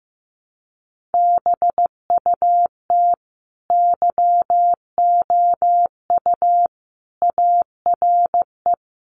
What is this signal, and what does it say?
Morse code, heard as a keyed tone, about 15 wpm, 700 Hz. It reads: BUT YOU ARE
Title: Sentences from Top 1200 Words 15wpm